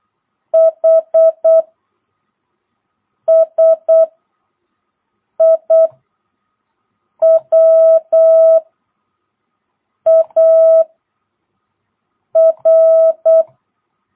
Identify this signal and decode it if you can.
Morse code, heard as a keyed tone, about 8 wpm, 660 Hz. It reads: H S I W A R
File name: jh8cbh / 2022cw